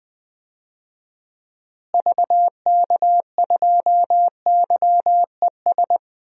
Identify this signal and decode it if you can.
Morse code, heard as a keyed tone, about 21 wpm, 700 Hz. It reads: VK2YES
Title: DXCC-100 Call Signs - 6 Characters - Repeated 2x - 20wpm